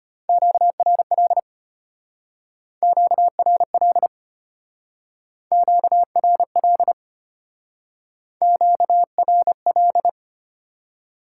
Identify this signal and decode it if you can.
Morse code, heard as a keyed tone, about 32 wpm, 700 Hz. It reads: QRL QRL QRL QRL